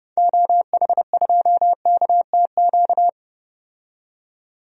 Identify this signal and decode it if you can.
Morse code, heard as a keyed tone, about 30 wpm, 700 Hz. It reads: OH2KTQ
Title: DXCC-100 Call Signs - 6 Characters - 30wpm